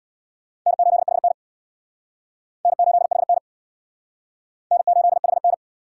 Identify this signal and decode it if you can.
Morse code, heard as a keyed tone, about 62 wpm, 700 Hz. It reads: N8HN N8HN N8HN
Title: Speed Racing - USA 1x2 Call Signs 50wpm